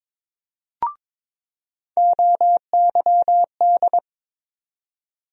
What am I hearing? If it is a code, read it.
Morse code, heard as a keyed tone, about 22 wpm, 700 Hz. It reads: OYD
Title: USA Call Sign Postfixes - 3 Letters 22wpm